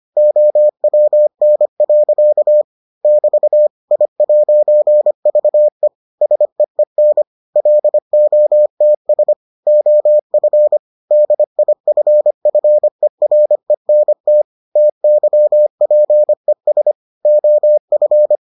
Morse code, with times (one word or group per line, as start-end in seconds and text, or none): 0.2-2.6 OWN.
3.0-3.7 <BT>
3.9-5.9 I'VE
6.2-7.2 SEEN
7.6-9.3 LOTS
9.7-10.8 OF
11.1-14.4 DIFFERENT
14.8-16.9 TYPES
17.3-18.4 OF